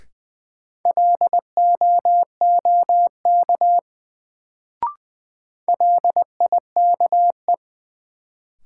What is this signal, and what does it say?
Morse code, heard as a keyed tone, about 20 wpm, 700 Hz. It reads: LOOK LIKE
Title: Sets of 2 Words - New Word Drill - 20wpm